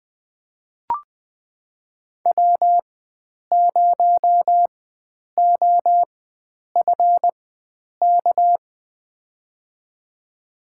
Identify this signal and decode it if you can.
Morse code, heard as a keyed tone, about 20 wpm, spaced as for 9 wpm, 700 Hz. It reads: W0OFK
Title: USA Call Signs - 5 Characters - 4x Character Spacing - 20wpm